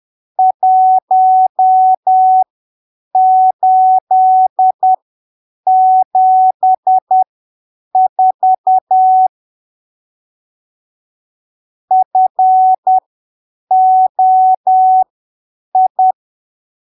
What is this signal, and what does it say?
Morse code, heard as a keyed tone, about 10 wpm, spaced as for 7 wpm, 750 Hz. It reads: 1874 FOI
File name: arquivos / texto / texto 16 10ppm